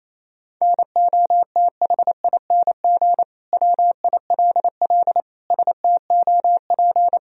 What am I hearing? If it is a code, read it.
Morse code, heard as a keyed tone, about 28 wpm, 700 Hz. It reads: NOTHING WILL STOP
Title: Sentences from Top 1000 Words - Review Part 1 - 28wpm